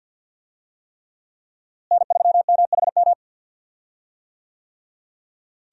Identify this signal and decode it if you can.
Morse code, heard as a keyed tone, about 52 wpm, 690 Hz. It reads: N3MHM
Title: USA Call Signs - 5 Characters - Repeated 3x - 50wpm